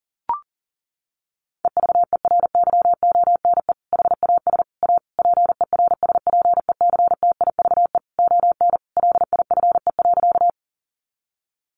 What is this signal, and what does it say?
Morse code, heard as a keyed tone, about 40 wpm, 700 Hz. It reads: EVERYONE HAS A PERSPECTIVE ON LIFE.
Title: Sentences from Top 1200 Words 40wpm